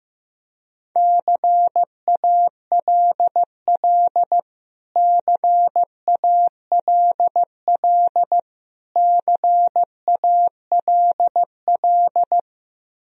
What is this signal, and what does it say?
Morse code, heard as a keyed tone, about 15 wpm, 700 Hz. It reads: CALL CALL CALL
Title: Top 100 Words Repeated 15wpm